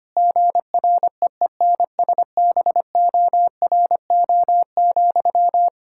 Morse code, heard as a keyed tone, about 25 wpm, 700 Hz. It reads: GREENSBORO,
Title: Practice QSO - Exchange 1 - Beginning - 25wpm